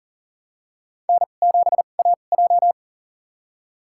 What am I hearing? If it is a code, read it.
Morse code, heard as a keyed tone, about 40 wpm, spaced as for 29 wpm, 700 Hz. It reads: N7AJ
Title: USA Call Signs - 4 Characters - 2x Character Spacing - 40wpm